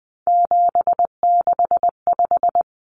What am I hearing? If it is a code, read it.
Morse code, heard as a keyed tone, about 20 wpm, 700 Hz. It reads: 765
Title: Numbers - 3-Digit 20wpm